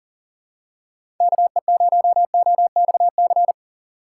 Morse code, heard as a keyed tone, about 40 wpm, 700 Hz. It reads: KE0OXC
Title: USA Call Signs - 6 Characters - 6x Character Spacing - 40wpm